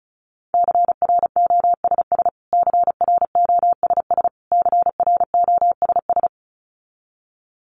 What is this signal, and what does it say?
Morse code, heard as a keyed tone, about 34 wpm, 700 Hz. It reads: CROSS CROSS CROSS
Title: Top 300 Words Repeated 35wpm